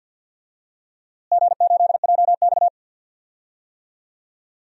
Morse code, heard as a keyed tone, about 52 wpm, 700 Hz. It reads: G8JX
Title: DXCC-100 Call Signs - 4 Characters - Repeated 2x - 50wpm